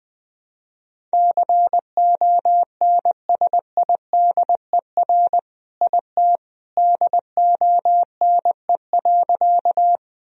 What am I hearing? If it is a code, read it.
Morse code, heard as a keyed tone, about 20 wpm, 700 Hz. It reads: CONSIDER IT DONE.
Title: Sentences from Top 1000 Words - Review Part 4 - 20wpm